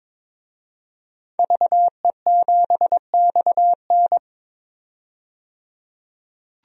Morse code, heard as a keyed tone, about 22 wpm, 700 Hz. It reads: VE7XN
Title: DXCC-100 Call Signs - 5 Characters - 22wpm